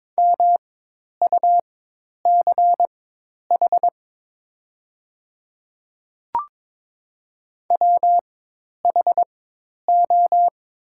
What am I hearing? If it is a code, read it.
Morse code, heard as a keyed tone, about 22 wpm, spaced as for 10 wpm, 700 Hz. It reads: MUCH WHO
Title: 4x Character Spacing - Top 300 Words - 22wpm